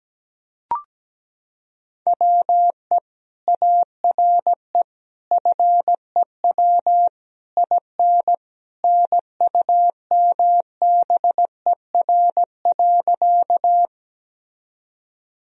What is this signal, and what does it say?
Morse code, heard as a keyed tone, about 17 wpm, 700 Hz. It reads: WE ARE FEW IN NUMBER.
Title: Sentences from Top 1000 Words - Review Part 1 - 17wpm